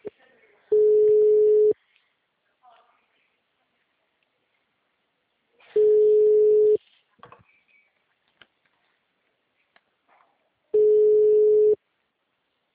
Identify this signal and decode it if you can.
Ringback tone (Europe, 425 Hz), with speech.